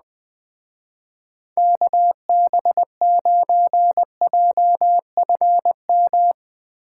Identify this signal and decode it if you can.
Morse code, heard as a keyed tone, about 20 wpm, 700 Hz. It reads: KB9JFM
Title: Speed Racing - USA Call Signs 17wpm